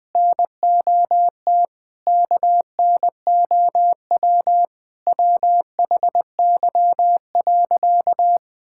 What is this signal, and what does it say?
Morse code, heard as a keyed tone, about 19 wpm, 700 Hz. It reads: NOT KNOW WHY.
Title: Sentences with 5-Word Phrases - 20wpm